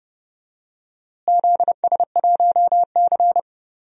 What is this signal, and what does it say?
Morse code, heard as a keyed tone, about 30 wpm, 700 Hz. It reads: ZS1C
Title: DXCC-100 Call Signs - 4 Characters - Repeated 3x - 30wpm